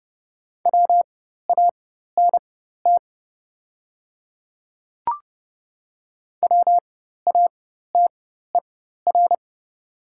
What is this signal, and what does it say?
Morse code, heard as a keyed tone, about 30 wpm, spaced as for 14 wpm, 700 Hz. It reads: WANT WATER
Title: 4x Character Spacing - Top 200 Words - 30wpm